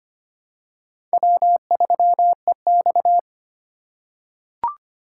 Morse code, heard as a keyed tone, about 25 wpm, 700 Hz. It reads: W3EX
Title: USA Call Signs - 4 Characters - 3x Character Spacing - 25wpm